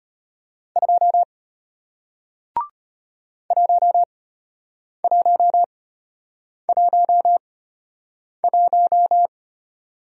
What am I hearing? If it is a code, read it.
Morse code, heard as a keyed tone, about 34 wpm, spaced as for 8 wpm, 700 Hz. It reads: 2 1111